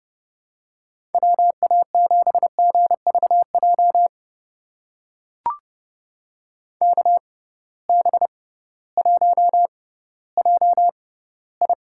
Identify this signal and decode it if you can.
Morse code, heard as a keyed tone, about 32 wpm, spaced as for 17 wpm, 700 Hz. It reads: WA7GVJ KB1JI
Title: USA Call Signs - 6 Characters - 6x Character Spacing - 30wpm